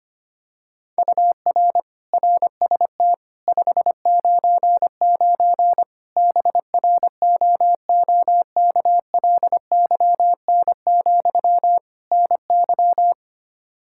Morse code, heard as a keyed tone, about 25 wpm, 700 Hz. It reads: UR RST 599 BROOKLYN, NY
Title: Practice QSO - Exchange 1 - Beginning - 25wpm